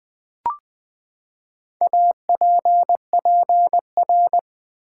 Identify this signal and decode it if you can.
Morse code, heard as a keyed tone, about 20 wpm, 700 Hz. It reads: APPR
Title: Build A Word - 4 Letters At A Time - 12-Letter Words - 20wpm